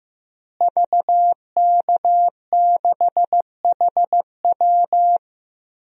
Morse code, heard as a keyed tone, about 15 wpm, 700 Hz. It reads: VK6HW
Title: DXCC-100 Call Signs - 5 Characters - Repeated 2x - 15wpm